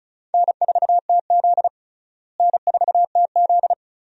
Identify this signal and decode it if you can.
Morse code, heard as a keyed tone, about 36 wpm, 700 Hz. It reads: N4TZ N4TZ
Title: USA Call Signs - 4 Characters - Repeated 2x - 35wpm